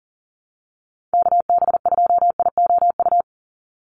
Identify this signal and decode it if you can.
Morse code, heard as a keyed tone, about 40 wpm, 700 Hz. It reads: KB2IOU
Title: USA Call Signs 40wpm